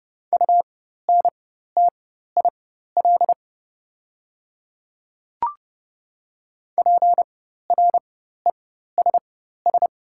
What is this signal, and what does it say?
Morse code, heard as a keyed tone, about 30 wpm, spaced as for 14 wpm, 700 Hz. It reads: UNTIL PRESS
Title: 4x Character Spacing - Top 300 Words - 30wpm